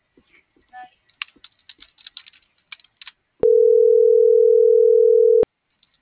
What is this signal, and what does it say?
Ringback tone (North America), 440 + 480 Hz.